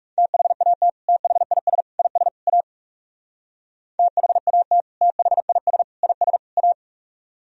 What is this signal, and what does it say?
Morse code, heard as a keyed tone, about 35 wpm, 700 Hz. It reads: THAT THIS IS A THAT THIS IS A
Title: Speed-Racing - Sets of 4 Words 30wpm